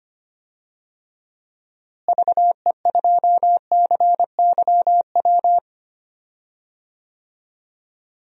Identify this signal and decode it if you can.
Morse code, heard as a keyed tone, about 25 wpm, 700 Hz. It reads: VE2CYW
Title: DXCC-100 Call Signs - 6 Characters - Repeated 3x - 25wpm